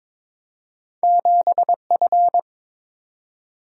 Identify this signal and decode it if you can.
Morse code, heard as a keyed tone, about 22 wpm, 700 Hz. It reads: 7F
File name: practice / 22wpm / Two Letter-Number Combinations - 22wpm